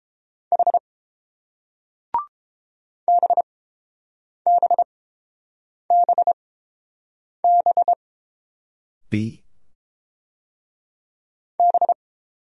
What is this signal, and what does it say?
Morse code, heard as a keyed tone, about 30 wpm, spaced as for 7 wpm, 700 Hz. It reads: H BBBB B